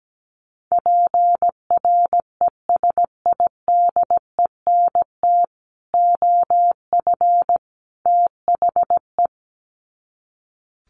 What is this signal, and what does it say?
Morse code, heard as a keyed tone, about 17 wpm, 700 Hz. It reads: PRESIDENT OF THE
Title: Sets of 3 Words 17wpm